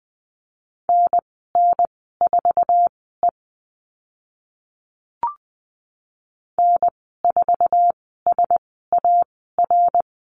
Morse code, heard as a keyed tone, about 20 wpm, spaced as for 15 wpm, 700 Hz. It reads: NN4E N4SAR
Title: 055 Lesson 10g - Letter r - Call Signs 20wpm